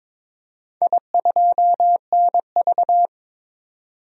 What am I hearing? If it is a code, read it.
Morse code, heard as a keyed tone, about 22 wpm, 700 Hz. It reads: I2NV
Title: DXCC-100 Call Signs - 4 Characters - 22wpm